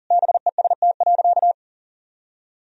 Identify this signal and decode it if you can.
Morse code, heard as a keyed tone, about 40 wpm, 700 Hz. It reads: BEST.
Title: Sentences from Top 1000 Words - Review Part 4 - 40wpm